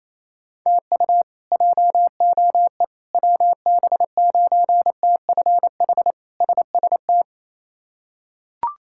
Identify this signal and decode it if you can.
Morse code, heard as a keyed tone, about 28 wpm, 700 Hz. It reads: TU JOE WB9TFH SST
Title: K1USN SST QSO Practice - 28wpm